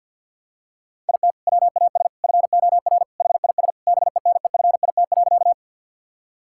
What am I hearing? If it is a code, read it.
Morse code, heard as a keyed tone, about 52 wpm, 700 Hz. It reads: IT WAS FOR HIS BENEFIT.
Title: Sentences from Top 1000 Words - Review Part 6 - 50wpm